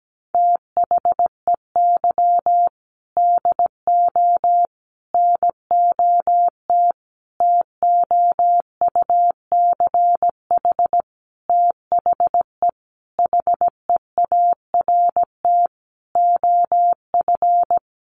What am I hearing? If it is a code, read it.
Morse code, heard as a keyed tone, about 17 wpm, 700 Hz. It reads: THEY DO NOT TOUCH THE HEART OF
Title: Sentences from Top 1200 Words 17wpm